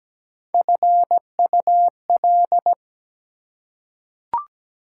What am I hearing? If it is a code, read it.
Morse code, heard as a keyed tone, about 17 wpm, 700 Hz. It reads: FUL
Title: Prefixes and Suffixes 17wpm